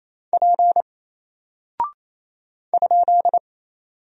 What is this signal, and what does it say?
Morse code, heard as a keyed tone, about 27 wpm, 700 Hz. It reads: P ?
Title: Single Character (Letter:Number:Punctuation:Prosign) 28wpm